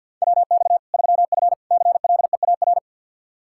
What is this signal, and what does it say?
Morse code, heard as a keyed tone, about 50 wpm, 700 Hz. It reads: WX 3F CLEAR
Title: K1USN SST QSO Practice 50wpm